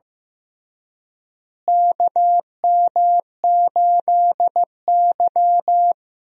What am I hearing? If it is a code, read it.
Morse code, heard as a keyed tone, about 15 wpm, 700 Hz. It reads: KM8Y
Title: USA Call Signs - 4 Characters - Repeated 2x - 15wpm